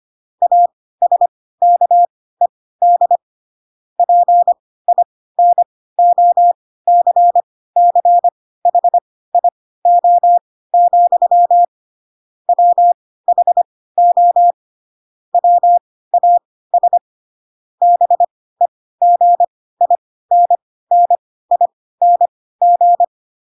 Morse code, heard as a keyed tone, about 25 wpm, spaced as for 16 wpm, 700 Hz. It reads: ASKED PINOCCHIO, WHO WAS BEGINNING
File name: Pinocchio / Pinocchio ch4 10 25 ik1hge